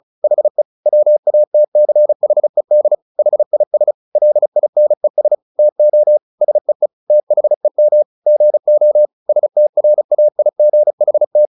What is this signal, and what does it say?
Morse code, heard as a keyed tone, about 35 wpm, 600 Hz. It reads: HE WATCHED HIS LINES TO SEE THEM GO STRAIGHT
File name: the old man and the sea / the old man and the sea 1 35wpm